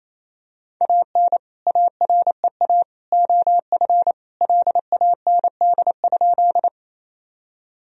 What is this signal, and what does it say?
Morse code, heard as a keyed tone, about 28 wpm, 700 Hz. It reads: AN AREA OF LAND?
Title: Sentences from Top 1000 Words - Review Part 2 - 28wpm